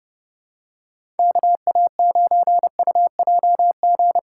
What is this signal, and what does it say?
Morse code, heard as a keyed tone, about 30 wpm, 700 Hz. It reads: KA9UJG
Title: USA Call Signs - 6 Characters - 3x Character Spacing - 30wpm